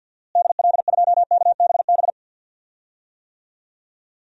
Morse code, heard as a keyed tone, about 52 wpm, 700 Hz. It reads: DL2KBB
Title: DXCC-100 Call Signs - 6 Characters - Repeated 3x - 50wpm